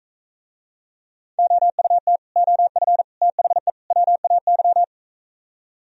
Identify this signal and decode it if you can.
Morse code, heard as a keyed tone, about 44 wpm, 700 Hz. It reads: OUT OF THE WAY